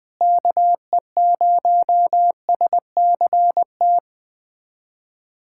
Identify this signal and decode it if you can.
Morse code, heard as a keyed tone, about 20 wpm, 700 Hz. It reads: KE0SCT
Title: USA Callsign DE Callsign - 6 Characters - 20wpm